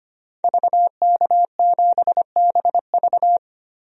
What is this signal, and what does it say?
Morse code, heard as a keyed tone, about 25 wpm, 700 Hz. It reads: VK7BV
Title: DXCC-100 Call Signs - 5 Characters - Repeated 2x - 25wpm